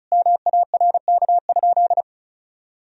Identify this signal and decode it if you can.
Morse code, heard as a keyed tone, about 34 wpm, 700 Hz. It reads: MARK?
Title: Sentences from Top 1000 Words - Review Part 1 - 35wpm